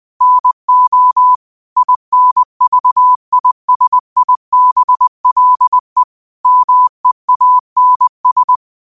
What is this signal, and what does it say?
Morse code, heard as a keyed tone, about 20 wpm, 1000 Hz. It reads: NO INVISIBLE MEANS
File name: qotdfiles / 20wpm